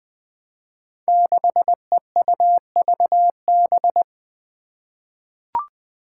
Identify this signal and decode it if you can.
Morse code, heard as a keyed tone, about 20 wpm, 700 Hz. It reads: 6EUVB
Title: Five Challenging Letter-Number Combinations 20wpm